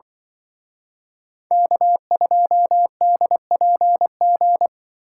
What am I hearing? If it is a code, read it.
Morse code, heard as a keyed tone, about 24 wpm, 700 Hz. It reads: K2DPG
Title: Speed Racing - USA Call Signs 20wpm